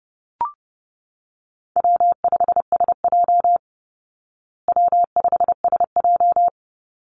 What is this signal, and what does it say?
Morse code, heard as a keyed tone, about 30 wpm, 700 Hz. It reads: W5SJ W5SJ
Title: Easy Call Signs Repeated 3x - 30wpm